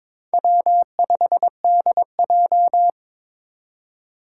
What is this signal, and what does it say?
Morse code, heard as a keyed tone, about 22 wpm, 700 Hz. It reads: W5DJ
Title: Four Challenging Letter-Number Combinations 22wpm